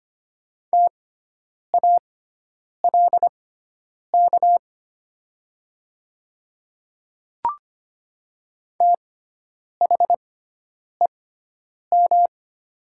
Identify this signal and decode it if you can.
Morse code, heard as a keyed tone, about 25 wpm, spaced as for 9 wpm, 700 Hz. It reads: TALK THEM
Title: 6x Character Spacing - Top 500 Words - 25wpm